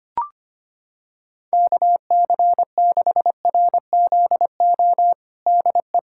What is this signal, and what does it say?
Morse code, heard as a keyed tone, about 25 wpm, 700 Hz. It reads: KC6RZO DE
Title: USA Callsign DE Callsign - 6 Characters - 25wpm